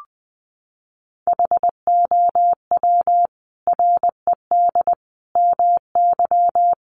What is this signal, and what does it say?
Morse code, heard as a keyed tone, about 19 wpm, 700 Hz. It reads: HOW RED MY
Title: Sentences from Top 1100 Words 20wpm